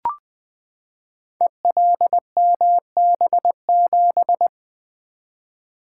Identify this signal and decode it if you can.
Morse code, heard as a keyed tone, about 21 wpm, 700 Hz. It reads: ELMB7
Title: Code Groups - 5 Letter:Numbers - 20wpm